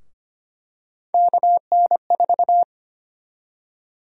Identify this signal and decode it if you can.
Morse code, heard as a keyed tone, about 25 wpm, 700 Hz. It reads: KN4
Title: USA Call Sign Prefixes - 25wpm